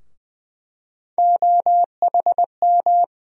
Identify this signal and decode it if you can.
Morse code, heard as a keyed tone, about 21 wpm, 700 Hz. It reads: OHM
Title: USA Call Sign Postfixes - 3 Letters 20wpm